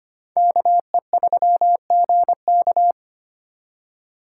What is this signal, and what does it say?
Morse code, heard as a keyed tone, about 32 wpm, 700 Hz. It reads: KE3GK